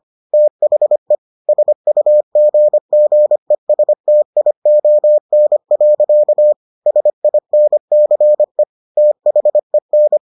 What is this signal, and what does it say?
Morse code, heard as a keyed tone, about 25 wpm, 600 Hz. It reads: THE SUGGESTION. SINCE THEN